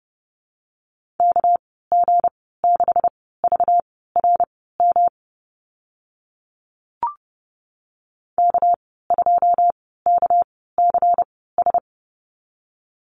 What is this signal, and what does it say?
Morse code, heard as a keyed tone, about 30 wpm, spaced as for 17 wpm, 700 Hz. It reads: KG6VRM K2KCS
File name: Lesson 37g - Letter x - Call Signs 30wpm